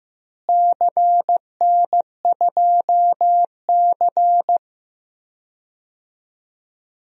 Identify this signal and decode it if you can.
Morse code, heard as a keyed tone, about 15 wpm, 700 Hz. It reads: CN2C
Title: DXCC-100 Call Signs - 4 Characters - Repeated 3x - 15wpm